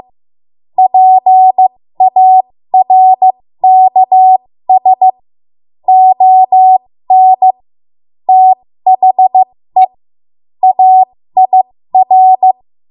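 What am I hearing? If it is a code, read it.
Morse code, heard as a keyed tone, about 15 wpm, spaced as for 13 wpm, 750 Hz. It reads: PARKS ON THE AIR
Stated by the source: 13 wpm Morse code practice archive transmission.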